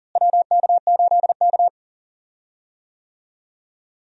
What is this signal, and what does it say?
Morse code, heard as a keyed tone, about 40 wpm, 700 Hz. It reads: WK8K
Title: USA Call Signs - 4 Characters - Repeated 2x - 40wpm